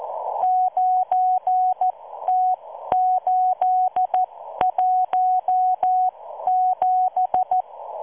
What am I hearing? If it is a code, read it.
Morse code, heard as a keyed tone, about 14 wpm, spaced as for 12 wpm, 740 Hz. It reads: 9T817